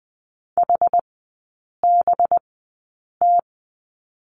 Morse code, heard as a keyed tone, about 20 wpm, spaced as for 8 wpm, 700 Hz. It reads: HBT
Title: 184 Lesson 30e - Letter b - Three Characters 20wpm